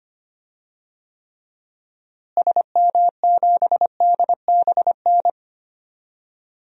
Morse code, heard as a keyed tone, about 25 wpm, 700 Hz. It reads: SM7DBN
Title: DXCC-100 Call Signs - 6 Characters - Repeated 3x - 25wpm